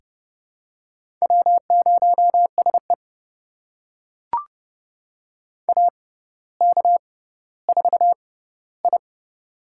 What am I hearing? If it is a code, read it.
Morse code, heard as a keyed tone, about 30 wpm, spaced as for 15 wpm, 700 Hz. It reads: W0SE AK4I
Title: USA Call Signs - 4 Characters - 6x Character Spacing - 30wpm